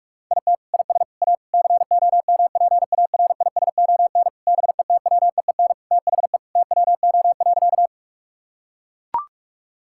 Morse code, heard as a keyed tone, about 41 wpm, 700 Hz. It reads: IT IS A COMPARISON BETWEEN THE TWO.
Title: Sentences from Top 1600 Words 45wpm